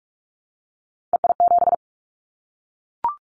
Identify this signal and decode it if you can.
Morse code, heard as a keyed tone, about 44 wpm, 700 Hz. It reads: EI7
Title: Callsign Prefixes for Top 50 DXCC - 45wpm